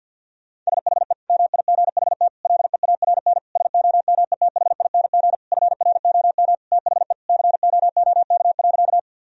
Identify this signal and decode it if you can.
Morse code, heard as a keyed tone, about 50 wpm, 700 Hz. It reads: SHE MIGHT LEARN SOMETHING FROM THE COOK.
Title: Sentences from Top 1000 Words - Review Part 4 - 50wpm